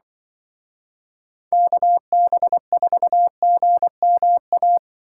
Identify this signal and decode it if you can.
Morse code, heard as a keyed tone, about 25 wpm, 700 Hz. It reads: KB4GMA
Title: Speed Racing - USA 2x3 Call Signs 20wpm